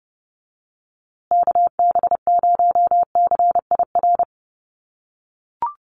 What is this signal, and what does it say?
Morse code, heard as a keyed tone, about 28 wpm, 700 Hz. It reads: KB0CIR
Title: USA Call Signs Repeated 3x 30wpm